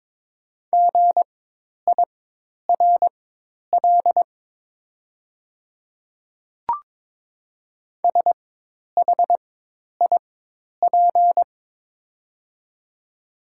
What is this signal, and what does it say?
Morse code, heard as a keyed tone, about 22 wpm, spaced as for 10 wpm, 700 Hz. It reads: GIRL SHIP
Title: 4x Character Spacing - Top 500 Words - 22wpm